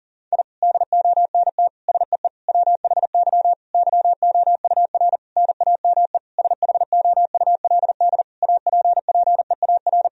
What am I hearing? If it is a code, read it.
Morse code, heard as a keyed tone, about 40 wpm, 700 Hz. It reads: I DONT SEE WHY YOUR NAME SHOULD APPEAR
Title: Sentences from Top 1000 Words - Review Part 2 - 40wpm